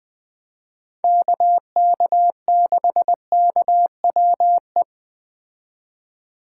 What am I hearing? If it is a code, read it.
Morse code, heard as a keyed tone, about 20 wpm, 700 Hz. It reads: KK6KWE